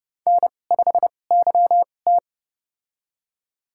Morse code, heard as a keyed tone, about 30 wpm, spaced as for 22 wpm, 700 Hz. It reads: N5YT
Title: USA Call Signs - 2x Character Spacing - 30wpm